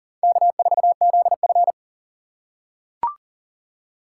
Morse code, heard as a keyed tone, about 40 wpm, 700 Hz. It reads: K4ZF
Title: USA Call Signs - 4 Characters - 6x Character Spacing - 40wpm